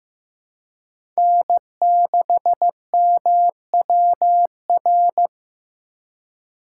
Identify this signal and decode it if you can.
Morse code, heard as a keyed tone, about 14 wpm, 700 Hz. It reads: N6MWR
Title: USA Call Signs - 5 Characters - 3x Character Spacing - 15wpm